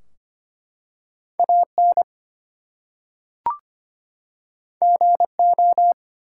Morse code, heard as a keyed tone, about 25 wpm, 700 Hz. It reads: AN GO
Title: Two Letter Words - 25wpm